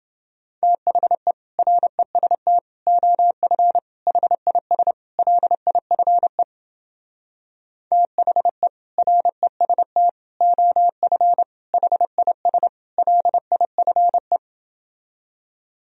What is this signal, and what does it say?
Morse code, heard as a keyed tone, about 30 wpm, 700 Hz. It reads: THE REST OF HIS LIFE THE REST OF HIS LIFE